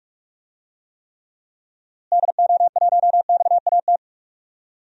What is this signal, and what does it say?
Morse code, heard as a keyed tone, about 43 wpm, 700 Hz. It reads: DO1XAT
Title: DXCC-100 Call Signs - 6 Characters - Repeated 2x - 45wpm